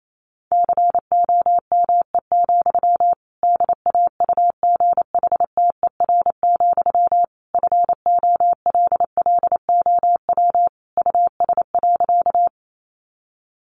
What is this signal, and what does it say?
Morse code, heard as a keyed tone, about 28 wpm, 700 Hz. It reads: COME, DAUGHTER, FOLLOW US.
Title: Sentences from Top 1500 Words 28wpm